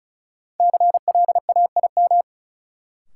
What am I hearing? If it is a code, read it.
Morse code, heard as a keyed tone, about 34 wpm, 700 Hz. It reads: CLAIM
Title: Sentences from Top 1500 Words - New Word Drill Alt - 35wpm